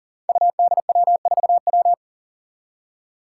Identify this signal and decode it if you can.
Morse code, heard as a keyed tone, about 40 wpm, 700 Hz. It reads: UDW4W
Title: Five Challenging Letter-Number Combinations 40wpm